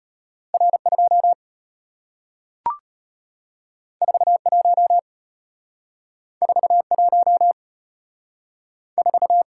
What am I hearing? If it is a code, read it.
Morse code, heard as a keyed tone, about 35 wpm, 700 Hz. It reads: R2 41 41 4